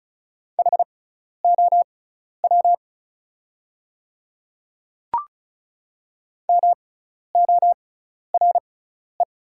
Morse code, heard as a keyed tone, about 34 wpm, spaced as for 12 wpm, 690 Hz. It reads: HOW MORE
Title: 6x Character Spacing - Top 200 Words - 35wpm